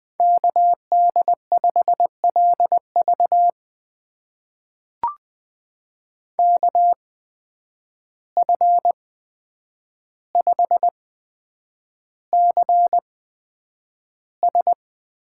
Morse code, heard as a keyed tone, about 20 wpm, spaced as for 9 wpm, 700 Hz. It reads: KD5LV KF5CS
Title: USA Call Signs - 5 Characters - 8x Character Spacing - 20wpm